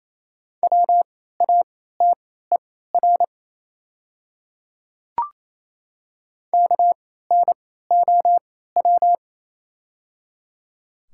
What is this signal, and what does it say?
Morse code, heard as a keyed tone, about 28 wpm, spaced as for 16 wpm, 700 Hz. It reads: WATER KNOW
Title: 3x Character Spacing - Top 200 Words - 28wpm